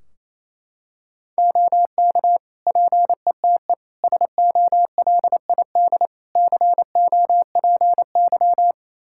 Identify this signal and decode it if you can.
Morse code, heard as a keyed tone, about 27 wpm, 700 Hz. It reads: OK PETE SOLID COPY
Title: Practice QSO - Exchange 3 - More Info - 28wpm